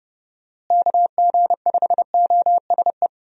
Morse code, heard as a keyed tone, about 30 wpm, 700 Hz. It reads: KG5OSE
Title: USA Call Signs - 6 Characters - Repeated 2x - 30wpm